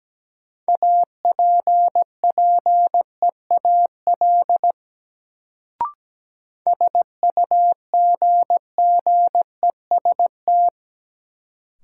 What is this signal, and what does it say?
Morse code, heard as a keyed tone, about 17 wpm, 700 Hz. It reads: APPEAL SUGGEST
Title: Sentences from Top 1300 Words - New Word Drill - 17wpm